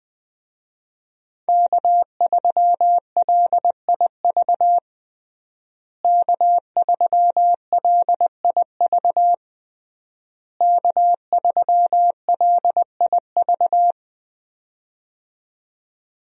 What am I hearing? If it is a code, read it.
Morse code, heard as a keyed tone, about 21 wpm, 700 Hz. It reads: K3LIV K3LIV K3LIV
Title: USA Call Signs - 5 Characters - Repeated 3x - 20wpm